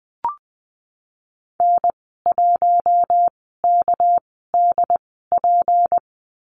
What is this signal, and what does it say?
Morse code, heard as a keyed tone, about 20 wpm, spaced as for 15 wpm, 700 Hz. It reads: N1KDP
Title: 213 Lesson 34g - Number 8 - Call Signs 20wpm